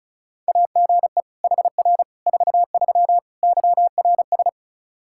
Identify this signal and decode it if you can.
Morse code, heard as a keyed tone, about 36 wpm, 700 Hz. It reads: AGE HR 43 YRS
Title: K1USN SST QSO Practice 35wpm